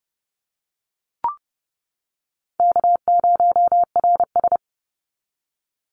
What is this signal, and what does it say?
Morse code, heard as a keyed tone, about 30 wpm, 700 Hz. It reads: K0RS
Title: NA SOTA Chaser Callsigns 30wpm